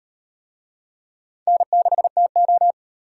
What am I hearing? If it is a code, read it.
Morse code, heard as a keyed tone, about 38 wpm, 700 Hz. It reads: N6TO